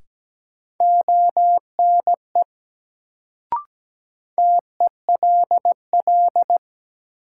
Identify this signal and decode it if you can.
Morse code, heard as a keyed tone, about 17 wpm, 700 Hz. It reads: ONE TELL
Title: Sets of 2 Words - New Word Drill - 17wpm